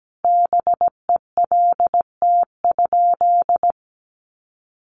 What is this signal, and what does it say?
Morse code, heard as a keyed tone, about 17 wpm, 700 Hz. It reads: BELT?
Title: Sentences from Top 1400 Words 17wpm